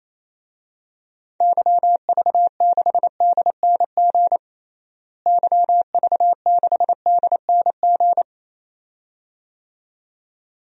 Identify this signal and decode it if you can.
Morse code, heard as a keyed tone, about 28 wpm, 700 Hz. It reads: YV6DNG YV6DNG
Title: DXCC-100 Call Signs - 6 Characters - Repeated 2x - 28wpm